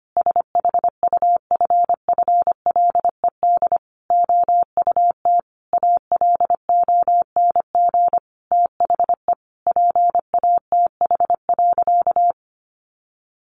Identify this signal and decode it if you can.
Morse code, heard as a keyed tone, about 25 wpm, 700 Hz. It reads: SHUFFLED OUT ALONG THE PATH.
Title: A Sound Of Thunder 25wpm